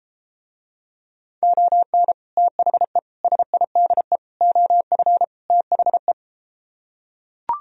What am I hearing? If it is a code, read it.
Morse code, heard as a keyed tone, about 33 wpm, 700 Hz. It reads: ON THE SIDE OF THE